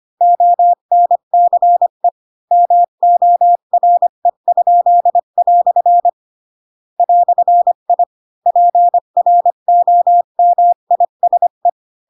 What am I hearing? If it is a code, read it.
Morse code, heard as a keyed tone, about 25 wpm, 700 Hz. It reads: ONCE MORE?" "I PROMISE
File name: Pinocchio / Pinocchio ch8 20 25 ik1hge